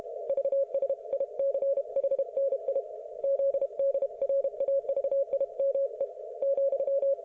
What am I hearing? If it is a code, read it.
Morse code, heard as a keyed tone, about 33 wpm, 560 Hz. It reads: VSICHNI ZDRAVIME ,